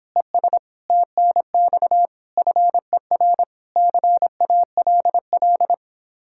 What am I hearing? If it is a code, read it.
Morse code, heard as a keyed tone, about 29 wpm, 700 Hz. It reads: ES TNX FER CALL